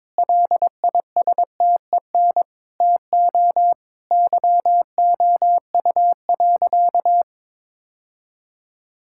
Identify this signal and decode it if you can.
Morse code, heard as a keyed tone, about 22 wpm, 700 Hz. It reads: LISTEN TO YOU.
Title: Sentences from Top 1600 Words 22wpm